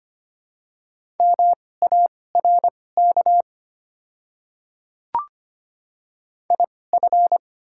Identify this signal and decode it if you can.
Morse code, heard as a keyed tone, about 25 wpm, spaced as for 18 wpm, 700 Hz. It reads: MARK IF
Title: 2x Character Spacing - Top 300 Words - 25wpm